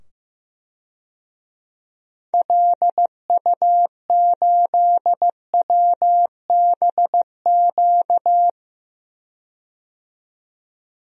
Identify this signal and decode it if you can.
Morse code, heard as a keyed tone, about 15 wpm, 700 Hz. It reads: LU8WBQ
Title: DXCC-100 Call Signs - 6 Characters - 15wpm